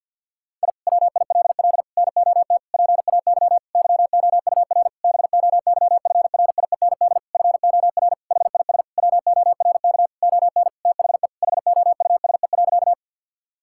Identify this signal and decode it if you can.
Morse code, heard as a keyed tone, about 52 wpm, 700 Hz. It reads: I WILL NOT PAY YOUR BOYFRIEND FOR HIS WORK ON THE HOUSE.
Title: Sentences from Top 1000 Words - Review Part 5 - 50wpm